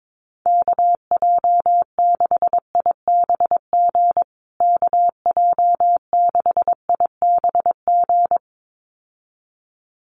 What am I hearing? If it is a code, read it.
Morse code, heard as a keyed tone, about 22 wpm, 700 Hz. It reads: KJ6IBG KJ6IBG
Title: USA Call Signs Repeated 2x 22wpm